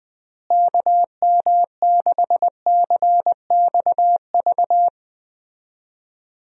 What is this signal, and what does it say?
Morse code, heard as a keyed tone, about 20 wpm, 700 Hz. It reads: KM6CXV
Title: USA Call Signs - 6 Characters - Repeated 3x - 20wpm